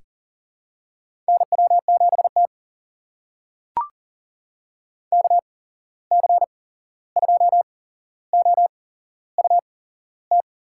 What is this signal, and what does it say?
Morse code, heard as a keyed tone, about 40 wpm, spaced as for 15 wpm, 700 Hz. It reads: NW7T KC2OUT
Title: USA Call Signs - 8x Character Spacing - 40wpm